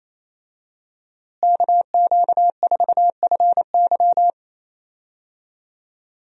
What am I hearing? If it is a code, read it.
Morse code, heard as a keyed tone, about 28 wpm, 700 Hz. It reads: KQ4FY
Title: USA Call Signs - 5 Characters - Repeated 2x - 28wpm